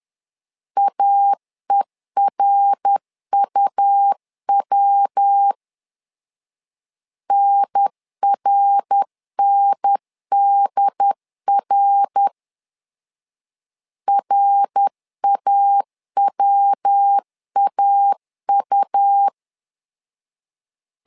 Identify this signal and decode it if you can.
Morse code, heard as a keyed tone, about 10 wpm, 800 Hz. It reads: AERUW NRNDR RAWAU
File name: LEZIONI / 3 MWRO / 3 MWR 10wpm 1952